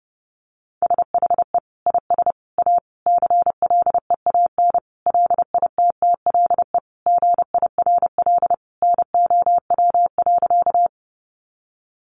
Morse code, heard as a keyed tone, about 28 wpm, 700 Hz. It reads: SHE IS A CLEAN LITTLE GIRL NOW.
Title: Sentences from Top 1100 Words 30wpm